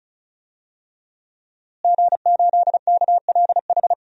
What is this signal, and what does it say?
Morse code, heard as a keyed tone, about 36 wpm, 700 Hz. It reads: G8KLH
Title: DXCC-100 Call Signs - 5 Characters - Repeated 3x - 35wpm